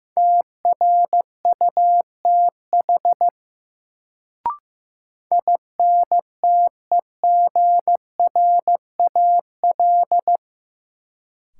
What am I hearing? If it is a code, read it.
Morse code, heard as a keyed tone, about 16 wpm, 700 Hz. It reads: TRUTH INTEGRAL
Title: Sets of 5 Words - New Word Drill - 15wpm